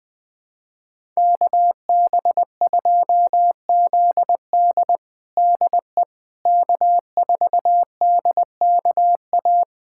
Morse code, heard as a keyed tone, about 21 wpm, 700 Hz. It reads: KB2ZD DE K4DKA
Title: USA Callsign DE Callsign - 5 Characters - 20wpm